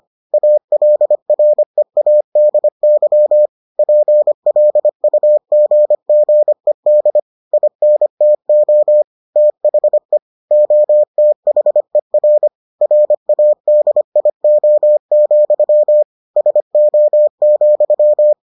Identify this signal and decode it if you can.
Morse code, heard as a keyed tone, about 25 wpm, 600 Hz. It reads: ALREADY PLUGGED INTO THE OTHER RADIO, SO,